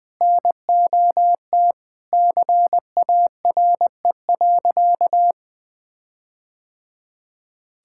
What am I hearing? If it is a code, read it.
Morse code, heard as a keyed tone, about 20 wpm, 700 Hz. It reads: NOT CARE.
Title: Sentences with 4-Word Phrases - 20wpm